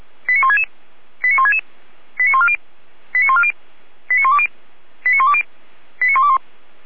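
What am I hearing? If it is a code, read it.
Five-tone selective call (ZVEI), heard as quick runs of five tones, tones of about 80 ms. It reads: 89166 89166 89140 89140 89120 89120 89121
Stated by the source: ZVEI five-tone selective call received on SDR.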